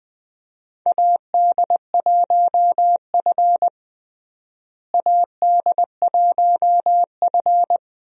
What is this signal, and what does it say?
Morse code, heard as a keyed tone, about 20 wpm, 700 Hz. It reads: AD1F AD1F
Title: USA Call Signs - 4 Characters - Repeated 2x - 20wpm